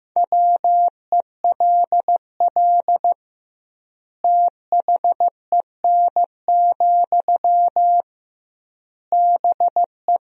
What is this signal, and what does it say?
Morse code, heard as a keyed tone, about 16 wpm, 700 Hz. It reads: WELL THEN, BE
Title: 2x Word-Spacing - Sentences from Top 500 Words - 15wpm